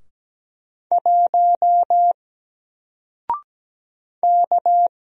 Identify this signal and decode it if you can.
Morse code, heard as a keyed tone, about 17 wpm, 700 Hz. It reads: 1 K
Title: Single Letter-Number Alt 17wpm